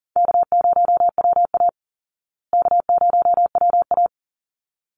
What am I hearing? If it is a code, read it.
Morse code, heard as a keyed tone, about 40 wpm, 700 Hz. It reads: K0WA K0WA
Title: Easy Call Signs Repeated 2x - 40wpm